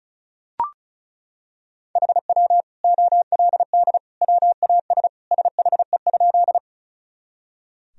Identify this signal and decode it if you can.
Morse code, heard as a keyed tone, about 36 wpm, 690 Hz. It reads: HW OLD WAS SHE?
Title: Sentences from Top 200 Words and Abbreviations 35wpm